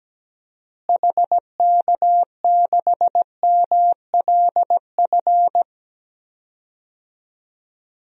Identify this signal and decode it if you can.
Morse code, heard as a keyed tone, about 17 wpm, 700 Hz. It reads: HK6MLF
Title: DXCC-100 Call Signs - 6 Characters - Repeated 2x - 17wpm